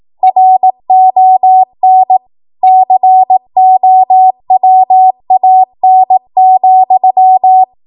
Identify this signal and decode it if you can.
Morse code, heard as a keyed tone, about 18 wpm, 750 Hz. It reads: RON COWAN,
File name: Morse / 18WPM